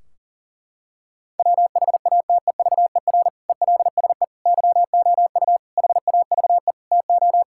Morse code, heard as a keyed tone, about 40 wpm, 700 Hz. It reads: WHATEVER ELSE YOU HAVE TO
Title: Sentences from Top 1600 Words 40wpm